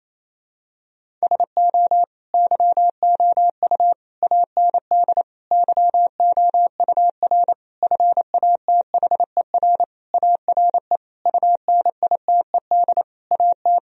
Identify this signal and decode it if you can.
Morse code, heard as a keyed tone, about 28 wpm, 700 Hz. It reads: SO YOU AND YOUR FATHER ARE UNITED AT
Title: Sentences from Top 1000 Words - Review Part 4 - 28wpm